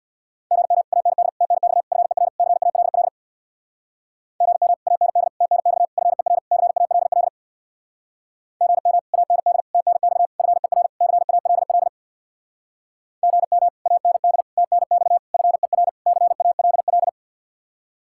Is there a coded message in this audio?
Yes: Morse code from 0.5 to 17.1 s, heard as a keyed tone, about 66 wpm, 700 Hz. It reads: GM AND TNX FER CALL GM AND TNX FER CALL GM AND TNX FER CALL GM AND TNX FER CALL